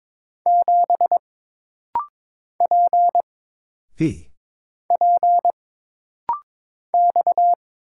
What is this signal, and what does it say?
Morse code, heard as a keyed tone, about 22 wpm, 700 Hz. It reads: 7 P P X
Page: This rapid-fire timing is geared to push your ICR (Instant Character Recognition) skills.